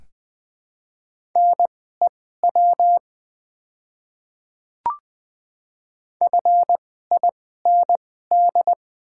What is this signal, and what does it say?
Morse code, heard as a keyed tone, about 20 wpm, spaced as for 14 wpm, 700 Hz. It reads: NEW FIND
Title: 2x Character Spacing - Top 300 Words - 20wpm